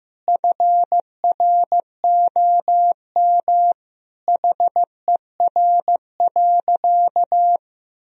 Morse code, heard as a keyed tone, about 15 wpm, 700 Hz. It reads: FROM HER.
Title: Sentences from Top 1000 Words - Review Part 2 - 15wpm